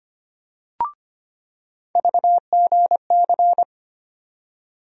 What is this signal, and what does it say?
Morse code, heard as a keyed tone, about 25 wpm, 700 Hz. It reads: VGC
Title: USA Call Sign Postfixes - 3 Letters 25wpm